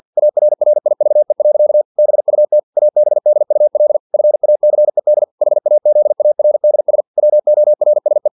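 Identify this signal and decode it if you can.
Morse code, heard as a keyed tone, about 50 wpm, 600 Hz. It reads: ARRIVE. BUT ABDUL FACED HAZARDS WORSE
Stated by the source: Morse practice text from an English-language novel at 49 wpm.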